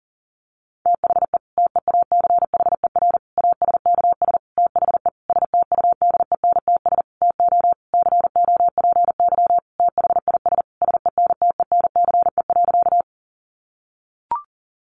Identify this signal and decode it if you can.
Morse code, heard as a keyed tone, about 40 wpm, 700 Hz. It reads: THE TEACHER ASKS THE STUDENTS TO COPY THIS SENTENCE.
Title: Additional Easy Phrases 40wpm